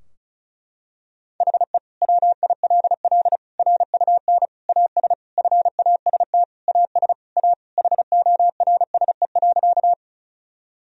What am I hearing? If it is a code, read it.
Morse code, heard as a keyed tone, about 36 wpm, 700 Hz. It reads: HE WILL RUN AS FAST AS A HORSE.
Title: Sentences from Top 1000 Words - Review Part 2 - 35wpm